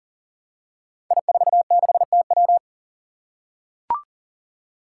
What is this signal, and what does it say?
Morse code, heard as a keyed tone, about 40 wpm, 700 Hz. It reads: I46TW
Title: Five Challenging Letter-Number Combinations 40wpm